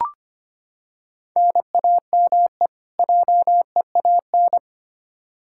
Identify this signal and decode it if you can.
Morse code, heard as a keyed tone, about 25 wpm, 700 Hz. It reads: NAME JEAN
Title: Practice QSO - Exchange 1 - Beginning - 25wpm